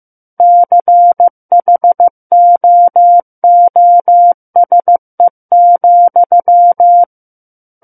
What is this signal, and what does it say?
Morse code, heard as a keyed tone, about 15 wpm, 700 Hz. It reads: CHOOSE,
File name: Book One / 06-Chapter 6 - 15 WPM